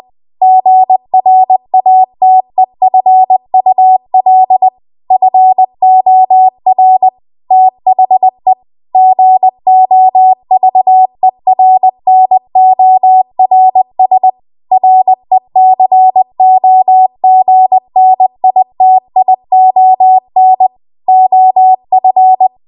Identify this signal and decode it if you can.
Morse code, heard as a keyed tone, about 20 wpm, 750 Hz. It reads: GRATEFUL FOR THE GOVERNORS RECOGNITION OF
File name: Morse / 20WPM